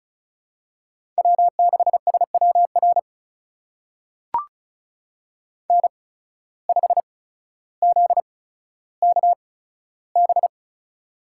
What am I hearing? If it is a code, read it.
Morse code, heard as a keyed tone, about 36 wpm, spaced as for 15 wpm, 700 Hz. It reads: W6SWR N5ZKB
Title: USA Call Signs - 5 Characters - 8x Character Spacing - 35wpm